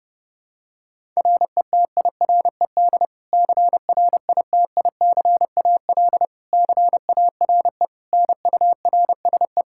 Morse code, heard as a keyed tone, about 32 wpm, 700 Hz. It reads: RETIRED CRITICAL CARE NURSE
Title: Practice QSO - Exchange 3 - More Info - 30wpm